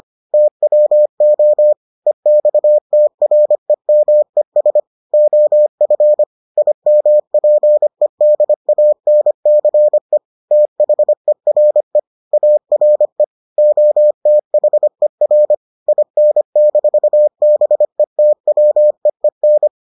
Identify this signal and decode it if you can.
Morse code, heard as a keyed tone, about 25 wpm, 600 Hz. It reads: TWO EXTREMES OF IMPEDANCE THERE ARE OTHER IN-BETWEEN